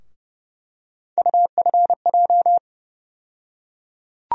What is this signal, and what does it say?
Morse code, heard as a keyed tone, about 30 wpm, 700 Hz. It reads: UFJ
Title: USA Call Sign Postfixes - 3 Letters 30wpm